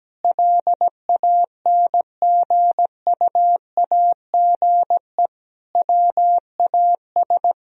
Morse code, heard as a keyed tone, about 17 wpm, 700 Hz. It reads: LANGUAGE WAS
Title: Sentences from Top 1000 Words - Review Part 4 - 17wpm